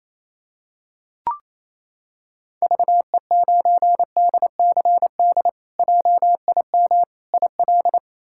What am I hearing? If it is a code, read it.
Morse code, heard as a keyed tone, about 28 wpm, 700 Hz. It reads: VE9DCD JIM IL
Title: K1USN SST QSO Practice - 28wpm